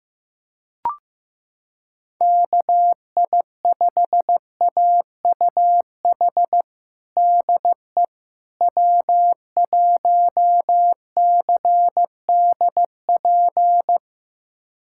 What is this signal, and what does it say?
Morse code, heard as a keyed tone, about 15 wpm, 700 Hz. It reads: KI5AUH DE W1CDP
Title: USA Callsign DE Callsign 15wpm